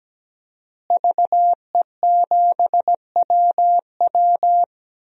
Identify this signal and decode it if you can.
Morse code, heard as a keyed tone, about 17 wpm, 700 Hz. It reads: VE7WW
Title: DXCC-100 Call Signs - 5 Characters - Repeated 3x - 17wpm